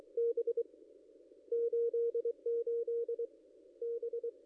Morse code, heard as a keyed tone, about 23 wpm, 470 Hz. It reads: B 88 B
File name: contests / ss-cw-k5tr-op-n5rz -0800